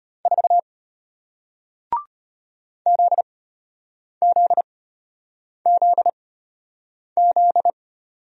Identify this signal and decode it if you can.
Morse code, heard as a keyed tone, about 33 wpm, spaced as for 8 wpm, 700 Hz. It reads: 4 ZZZZ